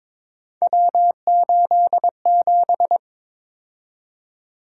Morse code, heard as a keyed tone, about 22 wpm, 700 Hz. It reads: W87
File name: practice / 22wpm / Three Letter-Number Combinations - 22wpm